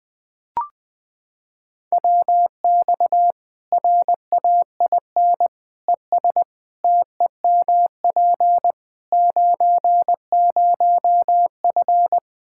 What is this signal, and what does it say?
Morse code, heard as a keyed tone, about 20 wpm, 700 Hz. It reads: WX RAIN ES TEMP 90F
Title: Practice QSO - Exchange 2 - Info - 20wpm